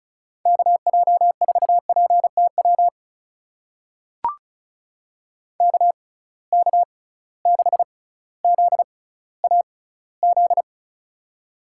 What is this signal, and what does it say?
Morse code, heard as a keyed tone, about 36 wpm, spaced as for 18 wpm, 700 Hz. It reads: KJ4PTW KK6ZAZ
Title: USA Call Signs - 6 Characters - 6x Character Spacing - 35wpm